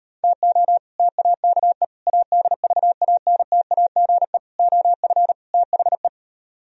Morse code, heard as a keyed tone, about 33 wpm, 700 Hz. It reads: TO TAKE ADVANTAGE OF THE